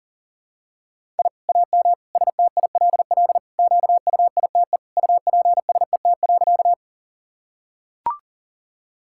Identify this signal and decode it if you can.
Morse code, heard as a keyed tone, about 37 wpm, 700 Hz. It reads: I AM STILL QUITE UPSET.
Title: Sentences from Top 1000 Words - Review Part 5 - 40wpm